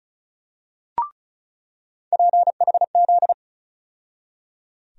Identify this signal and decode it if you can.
Morse code, heard as a keyed tone, about 36 wpm, 700 Hz. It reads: PHZ
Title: USA Call Sign Postfixes - 3 Letters 35wpm